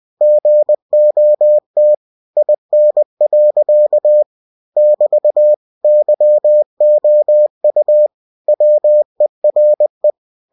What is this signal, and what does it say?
Morse code, heard as a keyed tone, about 20 wpm, 600 Hz. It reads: GOT IN. <BT> YOU WERE
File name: cw-books / winnie-the-pooh / winnie-20wpm-page-0015